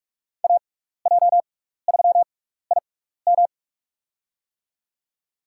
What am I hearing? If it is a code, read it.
Morse code, heard as a keyed tone, about 44 wpm, spaced as for 15 wpm, 700 Hz. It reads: AJ3IM
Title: USA Call Signs - 5 Characters - 6x Character Spacing - 45wpm